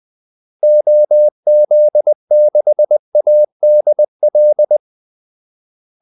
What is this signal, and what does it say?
Morse code, heard as a keyed tone, about 20 wpm, 600 Hz. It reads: OZ6ADL
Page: Kaldesignaler 20 wpm